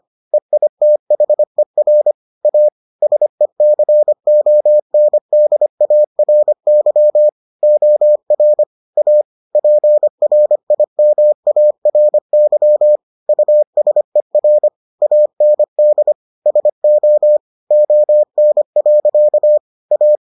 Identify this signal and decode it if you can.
Morse code, heard as a keyed tone, about 25 wpm, 600 Hz. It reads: EITHER A SECONDARY OR A PRIMARY USER AND SO ON. A